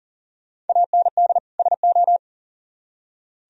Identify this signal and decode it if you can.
Morse code, heard as a keyed tone, about 50 wpm, 700 Hz. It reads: AND SO
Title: Speed-Racing - Sets of 2 Words 40wpm